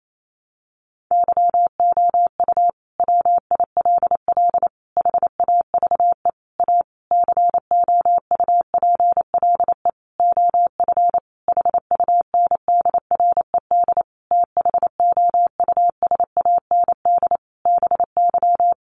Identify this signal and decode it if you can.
Morse code, heard as a keyed tone, about 28 wpm, 700 Hz. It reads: YOU WILL HAVE A COUPLE OF HUNDRED THOUSAND BY
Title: Sentences from Top 1100 Words 28wpm